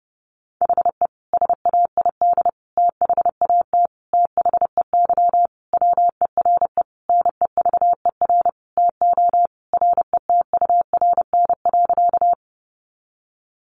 Morse code, heard as a keyed tone, about 30 wpm, 700 Hz. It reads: HE SAID THAT THEY WERE NEVER TO RETURN.
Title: Sentences from Top 800 Words 30wpm